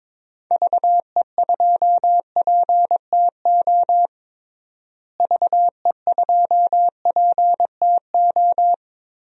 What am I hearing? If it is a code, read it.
Morse code, heard as a keyed tone, about 22 wpm, 700 Hz. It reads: VE2PTO VE2PTO
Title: DXCC-100 Call Signs - 6 Characters - Repeated 3x - 22wpm